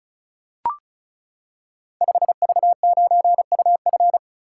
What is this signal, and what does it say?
Morse code, heard as a keyed tone, about 36 wpm, 700 Hz. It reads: 5V9UF
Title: Code Groups - 5 Letter:Numbers - 35wpm